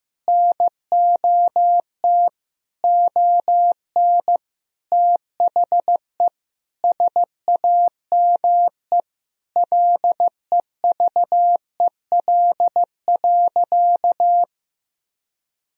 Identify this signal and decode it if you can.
Morse code, heard as a keyed tone, about 15 wpm, 700 Hz. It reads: NOT ON THE SAME LEVEL.
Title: Sentences from Top 1000 Words - Review Part 4 - 15wpm